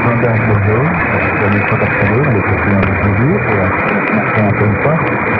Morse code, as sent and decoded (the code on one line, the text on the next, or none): none